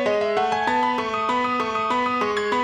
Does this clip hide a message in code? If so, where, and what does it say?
No.